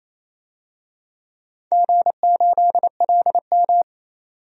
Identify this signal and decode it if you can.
Morse code, heard as a keyed tone, about 28 wpm, 700 Hz. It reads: G8LM
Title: DXCC-100 Call Signs - 4 Characters - 28wpm